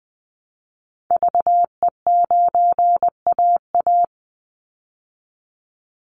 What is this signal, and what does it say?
Morse code, heard as a keyed tone, about 20 wpm, 700 Hz. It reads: VE9AA
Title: Call Signs Repeated 3x - 20wpm